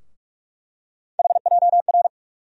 Morse code, heard as a keyed tone, about 46 wpm, 700 Hz. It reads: HJR
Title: USA Call Sign Postfixes - 3 Letters 45wpm